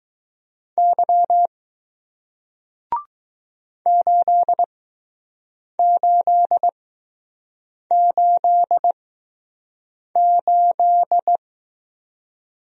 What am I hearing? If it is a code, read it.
Morse code, heard as a keyed tone, about 20 wpm, spaced as for 6 wpm, 700 Hz. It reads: Y 8888